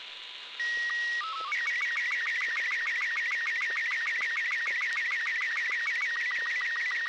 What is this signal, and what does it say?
SSTV (slow-scan television), Robot 36.
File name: arissat sstv green